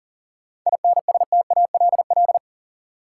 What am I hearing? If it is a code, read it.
Morse code, heard as a keyed tone, about 40 wpm, 700 Hz. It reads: INSTALL
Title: Seven Letter Words - 40wpm